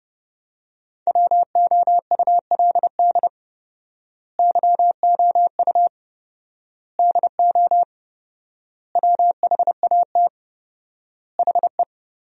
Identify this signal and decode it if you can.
Morse code, heard as a keyed tone, about 30 wpm, 700 Hz. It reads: WOULD YOU DO WHAT HE
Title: 4x Word-Spacing - Sentences from Top 100 Words - 30wpm